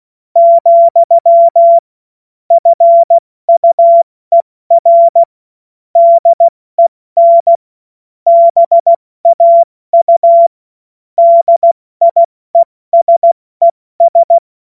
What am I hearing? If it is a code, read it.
Morse code, heard as a keyed tone, about 16 wpm, 680 Hz. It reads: , FUER DEN BAU DIESES
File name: Morsekurs / Witze / Witz 07.November 2023 70 F80